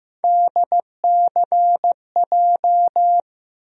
Morse code, heard as a keyed tone, about 15 wpm, 700 Hz. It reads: DCJ
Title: USA Call Sign Postfixes - 3 Letters 15wpm